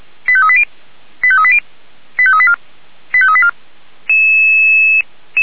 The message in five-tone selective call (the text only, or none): 85380 85380 85384 85384